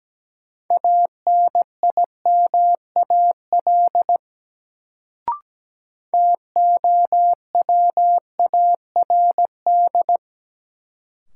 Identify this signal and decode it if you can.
Morse code, heard as a keyed tone, about 17 wpm, 700 Hz. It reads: ANIMAL TOWARD
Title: Sentences from Top 1300 Words - New Word Drill Alt - 17wpm